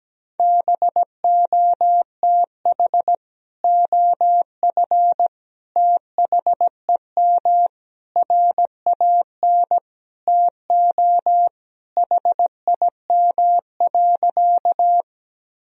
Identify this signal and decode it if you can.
Morse code, heard as a keyed tone, about 17 wpm, 700 Hz. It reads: BOTH OF THEM RAN TO HIM.
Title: Sentences from Top 1000 Words - Review Part 3 - 17wpm